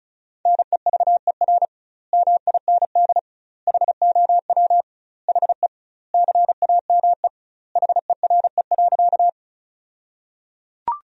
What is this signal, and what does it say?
Morse code, heard as a keyed tone, about 34 wpm, 700 Hz. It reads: NEVER MIND HOW HE CAME HERE.
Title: 2x Word-Spacing - Sentences from Top 500 Words - 35wpm